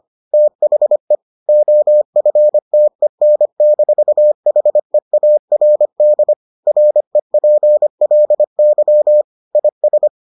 Morse code, heard as a keyed tone, about 25 wpm, 600 Hz. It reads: THE OFTEN-HEARD REPLY IS